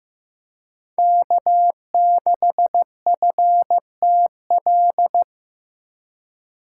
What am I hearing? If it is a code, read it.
Morse code, heard as a keyed tone, about 18 wpm, 700 Hz. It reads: K6FTL
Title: Speed Racing - USA 1x3 Call Signs 15wpm